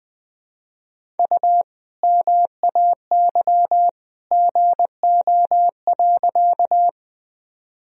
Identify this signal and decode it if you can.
Morse code, heard as a keyed tone, about 20 wpm, 700 Hz. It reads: U MAY GO.
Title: Sentences from Top 100 Words and Abbreviations 20wpm